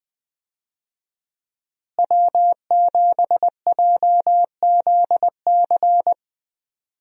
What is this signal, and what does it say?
Morse code, heard as a keyed tone, about 20 wpm, 700 Hz. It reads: W7JZC
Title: USA Call Signs - 5 Characters - Repeated 2x - 20wpm